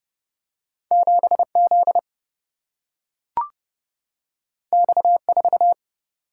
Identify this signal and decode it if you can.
Morse code, heard as a keyed tone, about 31 wpm, 700 Hz. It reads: 7Z X4
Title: Single-Letter Single-Number 30wpm